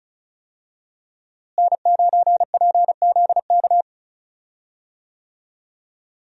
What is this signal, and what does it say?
Morse code, heard as a keyed tone, about 36 wpm, 700 Hz. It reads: N9PZK
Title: USA Call Signs - 5 Characters - Repeated 3x - 35wpm